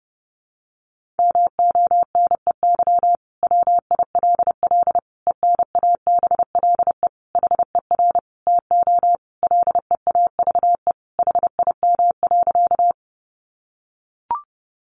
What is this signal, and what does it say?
Morse code, heard as a keyed tone, about 30 wpm, 700 Hz. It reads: MONEY WILL ENABLE HER TO LEAVE HIM.
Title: Sentences from Top 1500 Words 30wpm